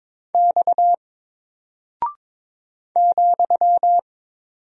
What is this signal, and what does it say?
Morse code, heard as a keyed tone, about 22 wpm, 700 Hz. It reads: X ,
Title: Single Character (Letter:Number:Punctuation:Prosign) 22wpm